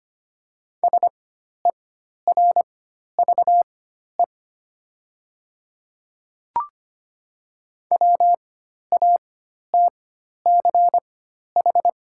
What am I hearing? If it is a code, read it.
Morse code, heard as a keyed tone, about 25 wpm, spaced as for 12 wpm, 700 Hz. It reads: SERVE WATCH
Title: 4x Character Spacing - Top 500 Words - 25wpm